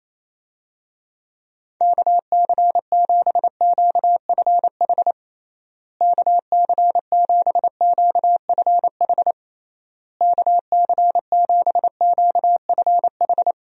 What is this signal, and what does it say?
Morse code, heard as a keyed tone, about 28 wpm, 700 Hz. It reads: KC7QFH KC7QFH KC7QFH
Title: USA Call Signs - 6 Characters - Repeated 3x - 28wpm